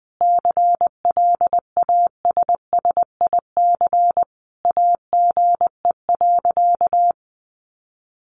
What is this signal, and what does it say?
Morse code, heard as a keyed tone, about 20 wpm, 700 Hz. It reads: CLASSIC AGE.
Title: Sentences from Top 1000 Words 20wpm